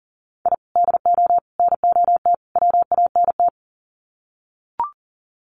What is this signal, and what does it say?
Morse code, heard as a keyed tone, about 40 wpm, 700 Hz. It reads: I DO NOT WANT
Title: Sets of 4 Words 40wpm